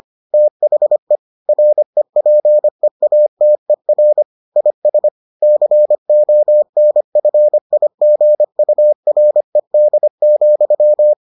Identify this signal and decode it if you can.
Morse code, heard as a keyed tone, about 25 wpm, 600 Hz. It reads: THE REPEATER IS CONFIGURED,